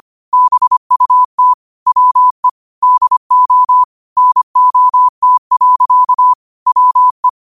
Morse code, heard as a keyed tone, about 25 wpm, 1000 Hz. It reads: BUT WE DO NOT. WE
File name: qotdfiles / 25wpm